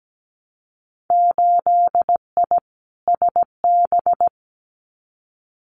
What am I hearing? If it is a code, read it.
Morse code, heard as a keyed tone, about 17 wpm, 700 Hz. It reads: 8I SB
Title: Winter Field Day Exchanges Encore 17wpm